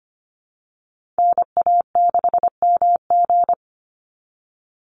Morse code, heard as a keyed tone, about 25 wpm, 700 Hz. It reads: NA6MG
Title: NA SOTA Activator Callsigns 25wpm